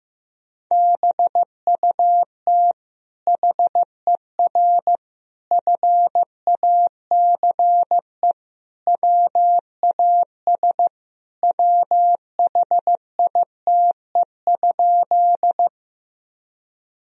Morse code, heard as a keyed tone, about 15 wpm, 700 Hz. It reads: BUT HER FACE WAS WHITE?
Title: Sentences from Top 1000 Words - Review Part 2 - 15wpm